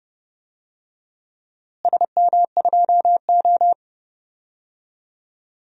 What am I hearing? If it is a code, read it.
Morse code, heard as a keyed tone, about 32 wpm, 700 Hz. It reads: SM2O
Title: DXCC-100 Call Signs - 4 Characters - Repeated 2x - 30wpm